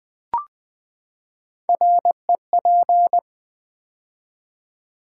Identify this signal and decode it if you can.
Morse code, heard as a keyed tone, about 20 wpm, 700 Hz. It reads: REP
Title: Build A Word - 3 Letters At A Time - 9-Letter Words - 20wpm